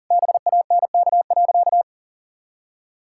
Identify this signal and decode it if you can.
Morse code, heard as a keyed tone, about 40 wpm, 700 Hz. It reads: BANK.
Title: Sentences from Top 1000 Words - Review Part 6 - 40wpm